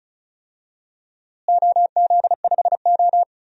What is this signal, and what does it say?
Morse code, heard as a keyed tone, about 36 wpm, 700 Hz. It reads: OZ5O
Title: DXCC-100 Call Signs - 4 Characters - Repeated 3x - 35wpm